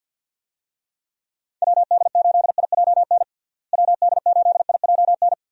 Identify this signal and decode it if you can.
Morse code, heard as a keyed tone, about 50 wpm, 700 Hz. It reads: WD8IJN WD8IJN
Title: USA Call Signs - 6 Characters - Repeated 2x - 50wpm